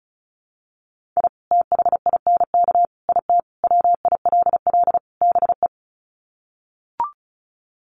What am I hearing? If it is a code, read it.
Morse code, heard as a keyed tone, about 34 wpm, 700 Hz. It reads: I THINK IT WILL BE
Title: Sets of 5 Words Encore2 35wpm